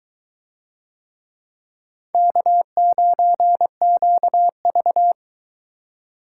Morse code, heard as a keyed tone, about 22 wpm, 700 Hz. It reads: K9QV